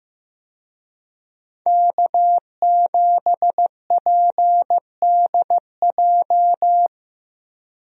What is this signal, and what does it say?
Morse code, heard as a keyed tone, about 15 wpm, 700 Hz. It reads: K7PDJ
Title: USA Call Signs - 5 Characters - 15wpm